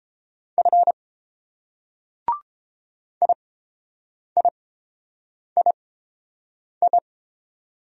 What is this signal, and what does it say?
Morse code, heard as a keyed tone, about 28 wpm, spaced as for 7 wpm, 690 Hz. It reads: F IIII